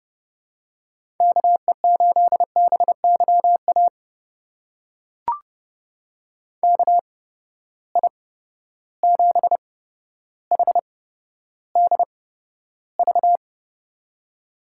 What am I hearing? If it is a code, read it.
Morse code, heard as a keyed tone, about 30 wpm, spaced as for 13 wpm, 700 Hz. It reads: KE8BYA KI7HDV
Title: USA Call Signs - 6 Characters - 8x Character Spacing - 30wpm